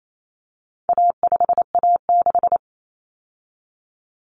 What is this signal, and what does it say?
Morse code, heard as a keyed tone, about 28 wpm, 700 Hz. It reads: A5A6
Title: Common Cut Numbers - 4-Digit 28wpm